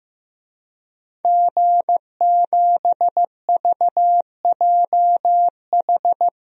Morse code, heard as a keyed tone, about 16 wpm, 700 Hz. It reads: G7VJH
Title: DXCC-100 Call Signs - 5 Characters - Repeated 3x - 15wpm